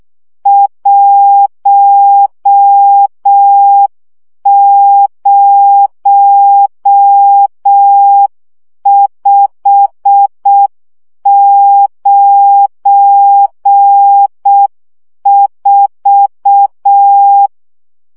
Morse code, heard as a keyed tone, about 6 wpm, 800 Hz. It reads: 10594
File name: morze vizsga 2011-1 5